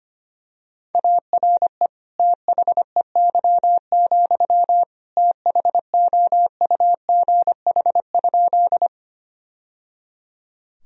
Morse code, heard as a keyed tone, about 25 wpm, 700 Hz. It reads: ARE THEY, THOUGH?
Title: Sentences from Top 1000 Words - Review Part 2 - 25wpm